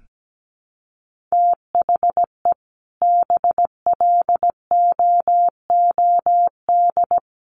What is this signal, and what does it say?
Morse code, heard as a keyed tone, about 17 wpm, 700 Hz. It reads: THE BLOOD
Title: Sentences from Top 1300 Words 17wpm